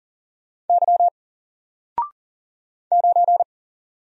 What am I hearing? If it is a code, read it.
Morse code, heard as a keyed tone, about 40 wpm, 700 Hz. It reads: Y 9
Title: Single Letter-Number Alt 40wpm